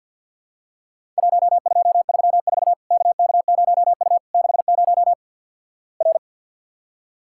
Morse code, heard as a keyed tone, about 50 wpm, 700 Hz. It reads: 1234 KK0U 60 R
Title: Worked All Europe DX Contest - QTC Exchange - 50wpm